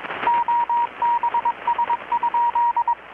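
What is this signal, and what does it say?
Morse code, heard as a keyed tone, about 21 wpm, 950 Hz, with radio noise. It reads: OBS?
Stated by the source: Morse from KPH's Night of Nights XX maritime radiotelegraphy event, received in Chiapas, Mexico about 3,800 km away on a Kenwood R-600 with a grounded 10 m long-wire antenna.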